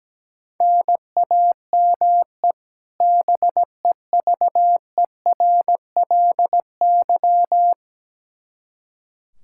Morse code, heard as a keyed tone, about 17 wpm, 700 Hz. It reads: NAME BEVERLY
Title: Practice QSO - Exchange 1 - Beginning - 17wpm